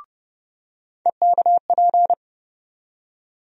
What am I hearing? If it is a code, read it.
Morse code, heard as a keyed tone, about 30 wpm, 700 Hz. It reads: EKP
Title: USA Call Sign Postfixes - 3 Letters 30wpm